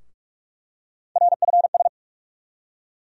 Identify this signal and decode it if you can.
Morse code, heard as a keyed tone, about 42 wpm, 700 Hz. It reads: RFS
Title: USA Call Sign Postfixes - 3 Letters 45wpm